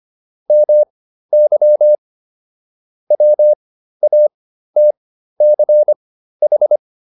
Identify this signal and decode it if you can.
Morse code, heard as a keyed tone, about 25 wpm, spaced as for 13 wpm, 600 Hz. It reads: MY WATCH
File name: PR405 13